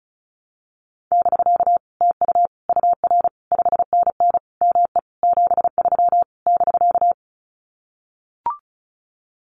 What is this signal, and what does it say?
Morse code, heard as a keyed tone, about 34 wpm, 700 Hz. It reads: <BK> TU UR 5NN ME 73 <BK>
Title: Common QSO Phrases Encore 35wpm